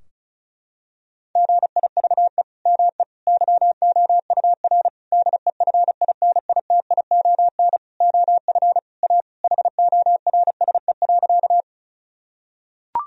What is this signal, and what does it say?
Morse code, heard as a keyed tone, about 34 wpm, 700 Hz. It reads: GIVE ME YOUR DEFINITION OF A HORSE.
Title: Sentences from Top 1000 Words - Review Part 4 - 35wpm